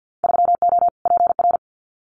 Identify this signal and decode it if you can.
Morse code, heard as a keyed tone, about 50 wpm, 700 Hz. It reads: 3O PR
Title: Winter Field Day Exchanges Encore 50wpm